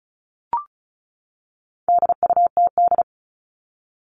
Morse code, heard as a keyed tone, about 34 wpm, 700 Hz. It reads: DUTD
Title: All Cut Numbers - 4-Digit 35wpm